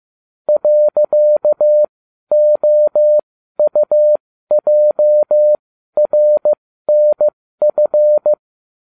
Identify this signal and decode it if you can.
Morse code, heard as a keyed tone, about 15 wpm, spaced as for 12 wpm, 600 Hz. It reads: .OUJRNF
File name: cw / ubacw20 3